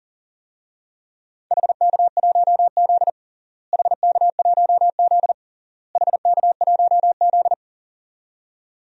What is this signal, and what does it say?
Morse code, heard as a keyed tone, about 40 wpm, 700 Hz. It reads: HK1Z HK1Z HK1Z
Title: DXCC-100 Call Signs - 4 Characters - Repeated 3x - 40wpm